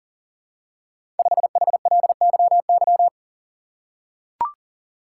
Morse code, heard as a keyed tone, about 39 wpm, 700 Hz. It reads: 5HLYY
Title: Code Groups - 5 Letter:Numbers - 40wpm